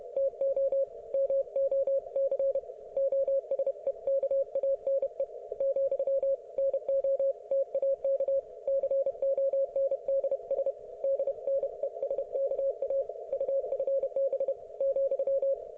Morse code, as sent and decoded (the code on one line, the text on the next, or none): - --- / -- --- -.-. / --- ... -.- .- -. / --..-- / -. --- - .- -.- / -.-. --- -. -.. ... / -.. -. ... -.- .- / ..- ..-. -... / --..--
TO MOC OSEKANE , NOTAK CONDS DNESKA UFB ,